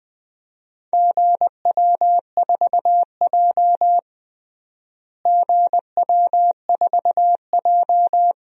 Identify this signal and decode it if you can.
Morse code, heard as a keyed tone, about 20 wpm, 700 Hz. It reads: GW4J GW4J
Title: DXCC-100 Call Signs - 4 Characters - Repeated 2x - 20wpm